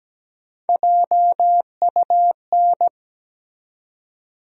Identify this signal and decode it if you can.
Morse code, heard as a keyed tone, about 17 wpm, 700 Hz. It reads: JUN
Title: USA Call Sign Postfixes - 3 Letters 17wpm